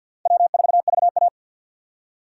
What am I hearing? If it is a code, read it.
Morse code, heard as a keyed tone, about 51 wpm, 700 Hz. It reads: W4VA
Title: USA Call Signs - 4 Characters - 3x Character Spacing - 50wpm